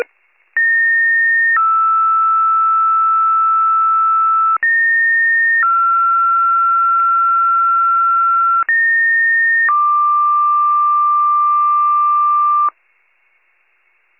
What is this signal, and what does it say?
Two-tone page, heard as a one-second tone, then a three-second tone, about 1850 then 1320 Hz, about 1850 then 1360 Hz, about 1850 then 1180 Hz. It is heard from 0.6 to 12.7 s.